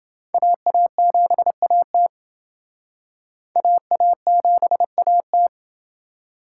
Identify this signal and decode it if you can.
Morse code, heard as a keyed tone, about 28 wpm, 700 Hz. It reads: AA7AT AA7AT